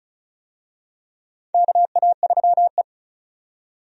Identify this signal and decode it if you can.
Morse code, heard as a keyed tone, about 36 wpm, 700 Hz. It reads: KA3E
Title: USA Call Signs - 4 Characters - 35wpm